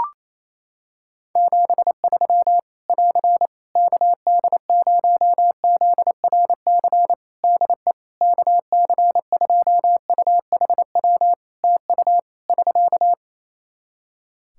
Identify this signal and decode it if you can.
Morse code, heard as a keyed tone, about 28 wpm, 700 Hz. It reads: 73 <AR> KD0ZRC DE KC2UHW TU <SK>
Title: Practice QSO - Exchange 4 - Ending - 28wpm